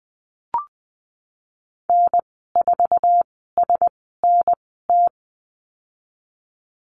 Morse code, heard as a keyed tone, about 20 wpm, spaced as for 15 wpm, 700 Hz. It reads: N4SNT
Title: 055 Lesson 10g - Letter r - Call Signs 20wpm